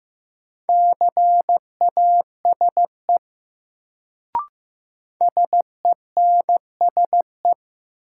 Morse code, heard as a keyed tone, about 15 wpm, 700 Hz. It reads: CASE SENSE
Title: Sentences from Top 700 Words - New Word Drill - 15wpm